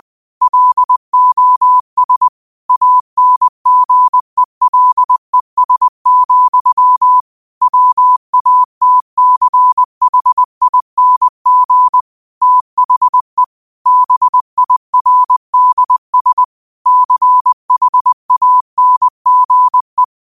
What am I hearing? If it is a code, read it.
Morse code, heard as a keyed tone, about 20 wpm, 1000 Hz. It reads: LOS ANGELES, WATCHING THE BIRDS CHANGE
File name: qotdfiles / 20wpm